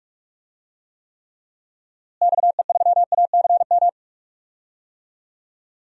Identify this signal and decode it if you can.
Morse code, heard as a keyed tone, about 46 wpm, 700 Hz. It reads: XE3ACM
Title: DXCC-100 Call Signs - 6 Characters - Repeated 3x - 45wpm